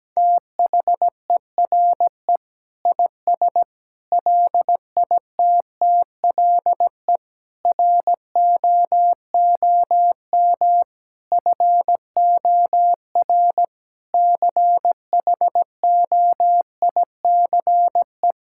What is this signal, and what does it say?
Morse code, heard as a keyed tone, about 17 wpm, 700 Hz. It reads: THERE IS LITTLE ROOM FOR CHOICE
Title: Sentences from Top 1000 Words - Review Part 6 - 17wpm